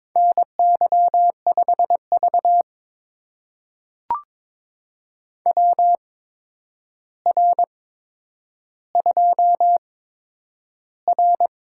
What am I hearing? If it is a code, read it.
Morse code, heard as a keyed tone, about 22 wpm, spaced as for 9 wpm, 700 Hz. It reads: NY5V WR2R
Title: USA Call Signs - 4 Characters - 8x Character Spacing - 22wpm